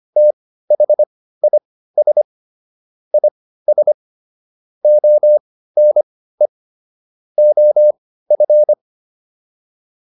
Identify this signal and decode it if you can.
Morse code, heard as a keyed tone, about 25 wpm, spaced as for 15 wpm, 600 Hz. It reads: THIS IS ONE OF